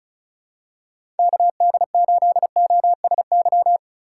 Morse code, heard as a keyed tone, about 36 wpm, 700 Hz. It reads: KD8OSY
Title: USA Call Signs - 6 Characters - 4x Character Spacing - 35wpm